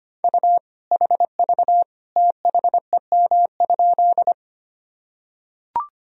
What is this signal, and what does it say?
Morse code, heard as a keyed tone, about 25 wpm, 700 Hz. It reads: U HV THEM?
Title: Sentences from Top 100 Words and Abbreviations 25wpm